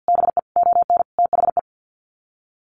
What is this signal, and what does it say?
Morse code, heard as a keyed tone, about 50 wpm, 700 Hz. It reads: BE ON THE
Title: Sets of 5 Words Encore2 50wpm